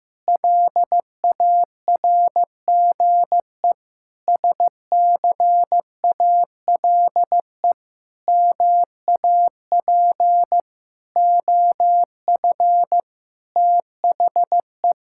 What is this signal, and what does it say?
Morse code, heard as a keyed tone, about 15 wpm, 700 Hz. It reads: LARGE SCALE MAP OF THE
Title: Sentences from Top 1000 Words - Review Part 5 - 15wpm